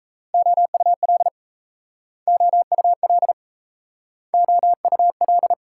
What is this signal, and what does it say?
Morse code, heard as a keyed tone, about 36 wpm, 700 Hz. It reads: OUL OUL OUL